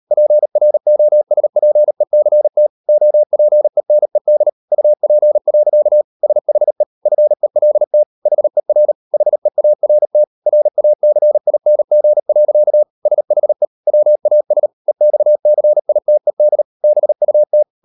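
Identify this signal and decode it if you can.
Morse code, heard as a keyed tone, about 38 wpm, 600 Hz. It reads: PROSPECT OPENED UP. SHE FELT HER HEART RACING. SHE WAS EXCITED BUT
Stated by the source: Morse code practice at 38 WPM, sending Ken Follett's novel Never in English, from CW med Gnister.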